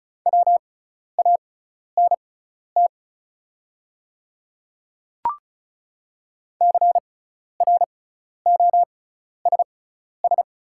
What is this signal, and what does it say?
Morse code, heard as a keyed tone, about 36 wpm, spaced as for 12 wpm, 700 Hz. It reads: WANT CROSS
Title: 6x Character Spacing - Top 300 Words - 35wpm